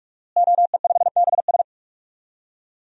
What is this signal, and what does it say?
Morse code, heard as a keyed tone, about 45 wpm, 700 Hz. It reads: OE5BS
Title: DXCC-100 Call Signs - 5 Characters - Repeated 3x - 45wpm